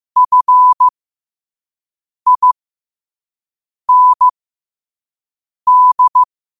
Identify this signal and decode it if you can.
Morse code, heard as a keyed tone, about 15 wpm, spaced as for 5 wpm, 1000 Hz. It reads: FIND
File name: qotdfiles / 5wpm